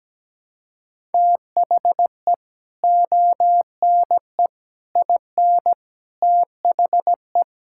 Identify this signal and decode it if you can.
Morse code, heard as a keyed tone, about 17 wpm, 700 Hz. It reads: THE ONE IN THE
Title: Sentences from Top 1600 Words 17wpm